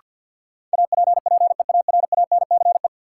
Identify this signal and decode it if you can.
Morse code, heard as a keyed tone, about 53 wpm, 700 Hz. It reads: APPEARANCE
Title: Sentences from Top 1100 Words - New Word Drill Alt - 50wpm